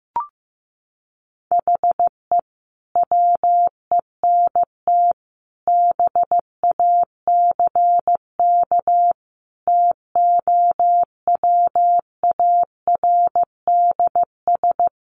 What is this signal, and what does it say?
Morse code, heard as a keyed tone, about 15 wpm, 700 Hz. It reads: HE WENT BACK TOWARDS
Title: Sentences from Top 1100 Words 15wpm